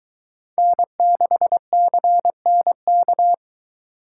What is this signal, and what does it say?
Morse code, heard as a keyed tone, about 22 wpm, 700 Hz. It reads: N6CNK